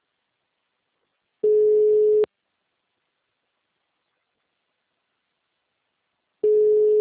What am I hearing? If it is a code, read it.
Ringback tone (Europe, 425 Hz).
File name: out- -2004- - - .35721